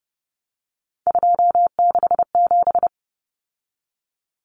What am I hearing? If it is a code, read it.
Morse code, heard as a keyed tone, about 30 wpm, 700 Hz. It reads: 267
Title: Common Cut Numbers - 3-Digit 30wpm